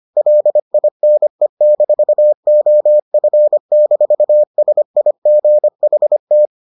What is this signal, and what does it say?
Morse code, heard as a keyed tone, about 25 wpm, 600 Hz. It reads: LINE-OF-SIGHT